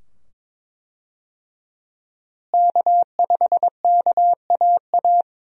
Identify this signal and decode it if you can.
Morse code, heard as a keyed tone, about 22 wpm, 700 Hz. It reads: K5KAA
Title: USA Call Signs - 5 Characters - Repeated 2x - 22wpm